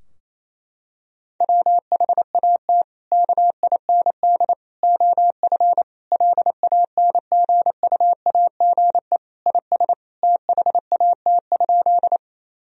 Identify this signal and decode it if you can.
Morse code, heard as a keyed tone, about 28 wpm, 700 Hz. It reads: WHAT KIND OF LANGUAGE IS THAT?
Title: Sentences from Top 1000 Words - Review Part 3 - 28wpm